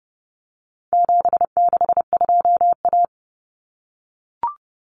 Morse code, heard as a keyed tone, about 30 wpm, 700 Hz. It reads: 762A
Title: Common Cut Numbers - 4-Digit 30wpm